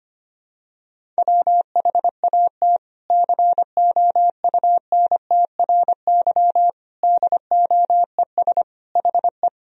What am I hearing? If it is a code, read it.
Morse code, heard as a keyed tone, about 25 wpm, 700 Hz. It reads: WHAT COUNTRY DOES HE
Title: Sentences from Top 300 Words and Abbreviations 25wpm